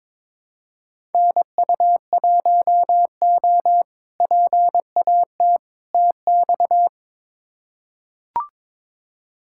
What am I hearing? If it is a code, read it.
Morse code, heard as a keyed tone, about 22 wpm, 700 Hz. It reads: NU1O PAT TX
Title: K1USN SST QSO Practice - 22wpm